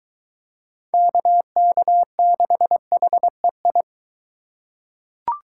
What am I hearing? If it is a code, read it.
Morse code, heard as a keyed tone, about 23 wpm, 700 Hz. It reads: KK6HEI